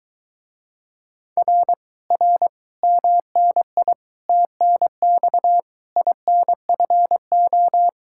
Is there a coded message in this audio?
Yes: Morse code from 1.4 to 7.9 s, heard as a keyed tone, about 22 wpm, 700 Hz. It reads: R R MNI TNX INFO